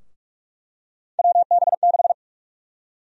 Morse code, heard as a keyed tone, about 45 wpm, 700 Hz. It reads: WB6
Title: USA Call Sign Prefixes - 45wpm